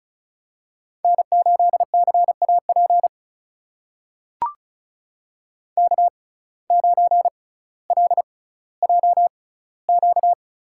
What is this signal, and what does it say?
Morse code, heard as a keyed tone, about 34 wpm, spaced as for 18 wpm, 700 Hz. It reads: N8CAP K9LJQ
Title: USA Call Signs - 5 Characters - 6x Character Spacing - 35wpm